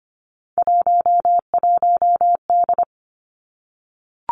Morse code, heard as a keyed tone, about 25 wpm, 700 Hz. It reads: 11D
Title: Three Challenging Letter-Number Combinations 25wpm